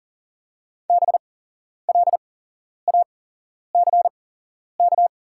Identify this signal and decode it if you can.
Morse code, heard as a keyed tone, about 40 wpm, spaced as for 11 wpm, 700 Hz. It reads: BLACK
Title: 8x Character Spacing - Top 500 Words - 40wpm